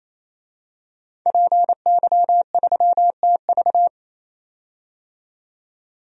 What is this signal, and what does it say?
Morse code, heard as a keyed tone, about 28 wpm, 700 Hz. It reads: PY3TV
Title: DXCC-100 Call Signs - 5 Characters - 28wpm